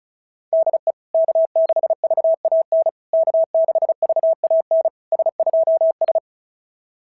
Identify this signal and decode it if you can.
Morse code, heard as a keyed tone, about 36 wpm, 660 Hz. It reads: DE K6VAN K6VAN S2S
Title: SOTA QSO Practice 35wpm